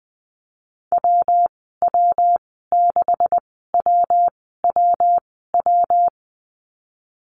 Morse code, heard as a keyed tone, about 20 wpm, spaced as for 14 wpm, 700 Hz. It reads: WW6WWW
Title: 140 Lesson 23g - Letter f - Call Signs 20wpm